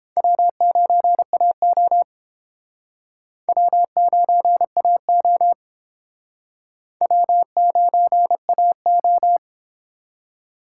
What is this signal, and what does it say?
Morse code, heard as a keyed tone, about 31 wpm, 700 Hz. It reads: W9AO W9AO W9AO